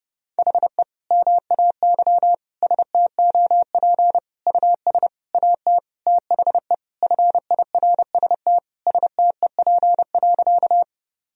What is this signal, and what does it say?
Morse code, heard as a keyed tone, about 30 wpm, 700 Hz. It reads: HE MAY STOP US AT THE FIRST STEP.
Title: Sentences from Top 1000 Words - Review Part 2 - 30wpm